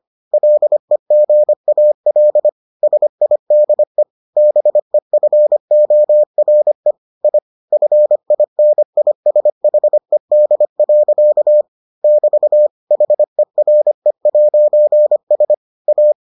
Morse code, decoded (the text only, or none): LEGAL SIDE BEFORE I FINISHED. <BT> HERE'S A